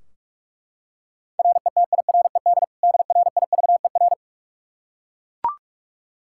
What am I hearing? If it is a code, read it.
Morse code, heard as a keyed tone, about 41 wpm, 700 Hz. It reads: RETIRED DRIVER
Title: K1USN SST QSO Practice 45wpm